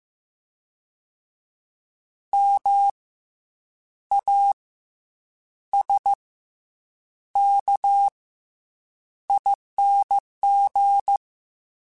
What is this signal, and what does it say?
Morse code, heard as a keyed tone, about 15 wpm, 800 Hz. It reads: M A S K ING
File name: Special INGs / Special INGs-mp3s / SlowToFaster ing @13